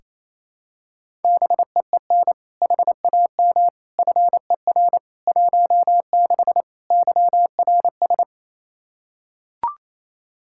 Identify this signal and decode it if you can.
Morse code, heard as a keyed tone, about 27 wpm, 700 Hz. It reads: BEEN HAM FER 16 YRS
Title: K1USN SST QSO Practice 28wpm